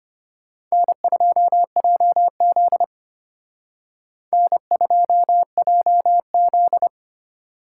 Morse code, heard as a keyed tone, about 32 wpm, 700 Hz. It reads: N2JZ N2JZ